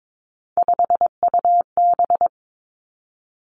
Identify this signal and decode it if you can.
Morse code, heard as a keyed tone, about 22 wpm, 700 Hz. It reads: 5UB
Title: Three Challenging Letter-Number Combinations 22wpm